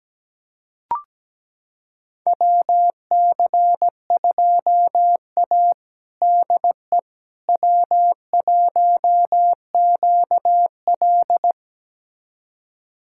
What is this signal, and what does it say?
Morse code, heard as a keyed tone, about 17 wpm, 700 Hz. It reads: WC2A DE W1QL
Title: USA Callsign DE Callsign - 4 Characters - 17wpm